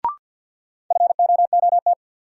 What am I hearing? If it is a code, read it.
Morse code, heard as a keyed tone, about 52 wpm, 700 Hz. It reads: FOOT
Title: Sets of 5 Words - New Word Drill - 50wpm